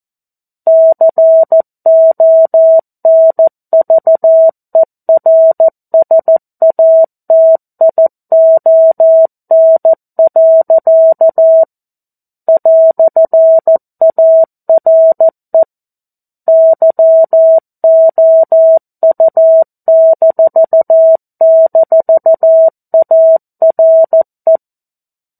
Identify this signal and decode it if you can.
Morse code, heard as a keyed tone, about 14 wpm, 650 Hz. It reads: CONVERSATION. "ARE YOU--ARE
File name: 02-Chapter-02 14 WPM